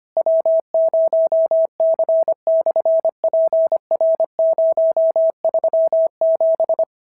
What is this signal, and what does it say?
Morse code, heard as a keyed tone, about 25 wpm, 650 Hz. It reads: W0C/PR037
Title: SOTA QSO Practice 25wpm